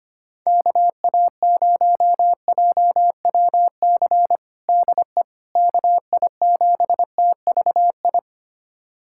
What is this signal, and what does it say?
Morse code, heard as a keyed tone, about 25 wpm, 700 Hz. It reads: KA0JWC DE KI7TVI
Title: USA Callsign DE Callsign - 6 Characters - 25wpm